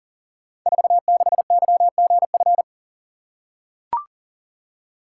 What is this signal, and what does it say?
Morse code, heard as a keyed tone, about 39 wpm, 700 Hz. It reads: 46YGF
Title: Code Groups - 5 Letter:Numbers - 40wpm